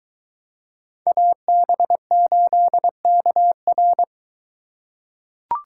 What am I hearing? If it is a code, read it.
Morse code, heard as a keyed tone, about 22 wpm, 700 Hz. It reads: AB8KR